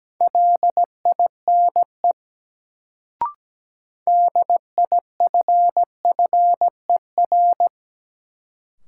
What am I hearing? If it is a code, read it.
Morse code, heard as a keyed tone, about 17 wpm, 700 Hz. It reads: LINE DIFFER
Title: Sentences from Top 1000 Words - Review Part 1 - New Word Drill - 17wpm